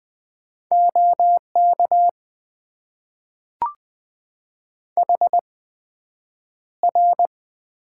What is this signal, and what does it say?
Morse code, heard as a keyed tone, about 20 wpm, spaced as for 9 wpm, 700 Hz. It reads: OK HR
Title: Top 90 QSO Elements - 8x Character Spacing - 20wpm